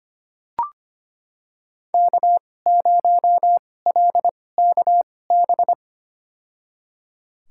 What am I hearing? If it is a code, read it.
Morse code, heard as a keyed tone, about 25 wpm, spaced as for 18 wpm, 700 Hz. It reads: K0LKB
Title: USA Call Signs - 5 Characters - 2x Character Spacing - 25wpm